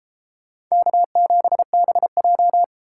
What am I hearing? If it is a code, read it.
Morse code, heard as a keyed tone, about 37 wpm, 700 Hz. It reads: K7BJ